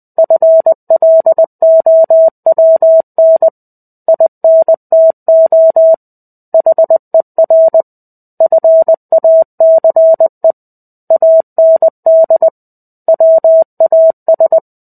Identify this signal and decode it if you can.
Morse code, heard as a keyed tone, about 20 wpm, 650 Hz. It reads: FLOWN INTO HER FACE AND WAS
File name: 05-Chapter-05 20 WPM